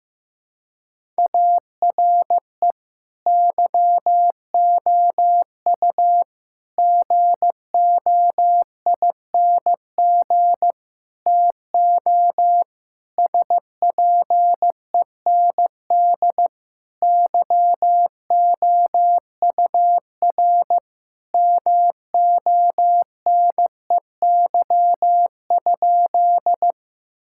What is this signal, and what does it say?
Morse code, heard as a keyed tone, about 15 wpm, 700 Hz. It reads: ARE YOU GOING TO SPEND YOUR MONEY?
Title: Sentences from Top 1000 Words - Review Part 5 - 15wpm